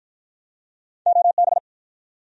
Morse code, heard as a keyed tone, about 48 wpm, 700 Hz. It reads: OB